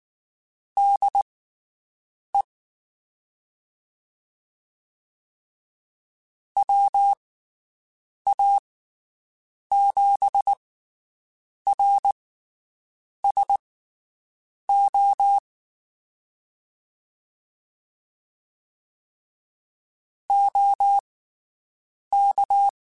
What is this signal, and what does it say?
Morse code, heard as a keyed tone, about 19 wpm, spaced as for 7 wpm, 800 Hz. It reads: DE WA7RSO OK
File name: QSOs Spaced